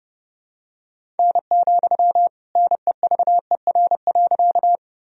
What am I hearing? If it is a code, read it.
Morse code, heard as a keyed tone, about 30 wpm, 700 Hz. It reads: N, NEVER.
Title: Sentences from Top 300 Words and Abbreviations 30wpm